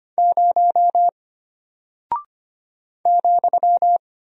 Morse code, heard as a keyed tone, about 25 wpm, 700 Hz. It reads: 0 ,
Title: Single Character (Letter:Number:Punctuation:Prosign) Alt 25wpm